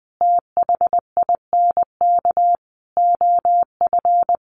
Morse code, heard as a keyed tone, about 20 wpm, 700 Hz. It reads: THINK OF
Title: Sentences from Top 400 Words 20wpm